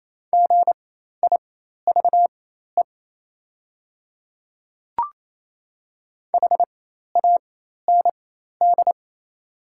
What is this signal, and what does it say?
Morse code, heard as a keyed tone, about 28 wpm, spaced as for 13 wpm, 700 Hz. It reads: GIVE HAND
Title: 4x Character Spacing - Top 200 Words - 28wpm